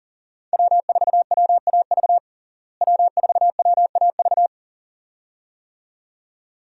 Morse code, heard as a keyed tone, about 37 wpm, 700 Hz. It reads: W4WAV W4WAV
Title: USA Call Signs - 5 Characters - Repeated 2x - 40wpm